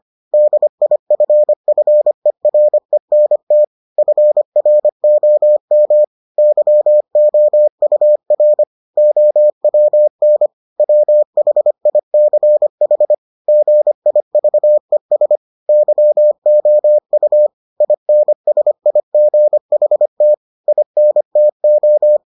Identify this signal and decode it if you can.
Morse code, heard as a keyed tone, about 25 wpm, 600 Hz. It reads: DIFFERENT FROM YOUR OWN WHICH GIVES YOU INSIGHT INTO